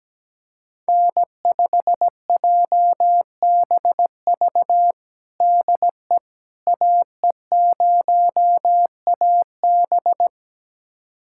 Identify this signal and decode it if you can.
Morse code, heard as a keyed tone, about 17 wpm, 700 Hz. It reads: N5JBV DE AE0AB
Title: USA Callsign DE Callsign - 5 Characters - 17wpm